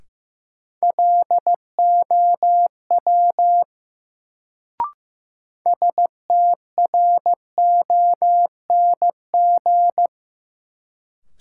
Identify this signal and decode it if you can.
Morse code, heard as a keyed tone, about 15 wpm, 700 Hz. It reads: LOW STRONG
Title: Sentences from Top 500 Words - New Word Drill - 15wpm